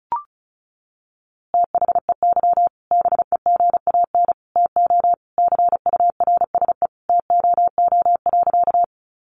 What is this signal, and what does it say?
Morse code, heard as a keyed tone, about 34 wpm, 700 Hz. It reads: THEY BEGAN TO CURSE TOO.
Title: A Sound Of Thunder 35wpm